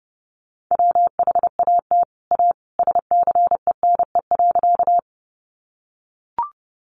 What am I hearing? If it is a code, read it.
Morse code, heard as a keyed tone, about 30 wpm, 700 Hz. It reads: WHAT A SCENE.
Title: Sentences from Top 1100 Words 30wpm